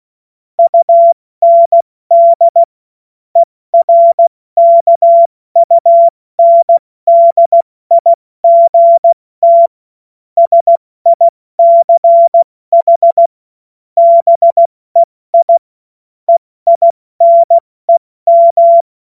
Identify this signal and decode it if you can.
Morse code, heard as a keyed tone, about 16 wpm, 680 Hz. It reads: UND ERKUNDIGT SICH BEI EINEM
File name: Morsekurs / Witze / Witz 07. Februar 2023 70 F80